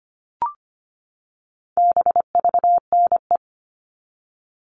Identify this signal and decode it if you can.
Morse code, heard as a keyed tone, about 25 wpm, 700 Hz. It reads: BVNE
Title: All Cut Numbers - 4-Digit 25wpm